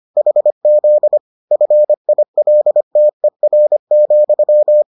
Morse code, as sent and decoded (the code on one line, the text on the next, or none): .... --.. / ..-. .. .-.. - . .-. --..--
HZ FILTER,